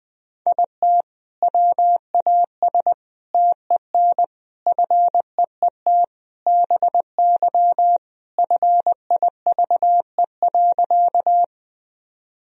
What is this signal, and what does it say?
Morse code, heard as a keyed tone, about 20 wpm, 700 Hz. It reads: IT WAS TEN FEET BY FIVE.
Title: Sentences from Top 1000 Words - Review Part 2 - 20wpm